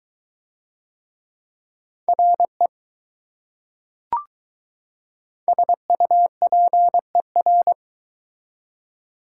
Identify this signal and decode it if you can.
Morse code, heard as a keyed tone, about 21 wpm, 700 Hz. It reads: RE SUPER